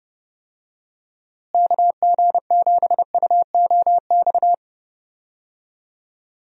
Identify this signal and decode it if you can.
Morse code, heard as a keyed tone, about 30 wpm, 700 Hz. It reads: KG7UOX
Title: USA Call Signs - 6 Characters - Repeated 2x - 30wpm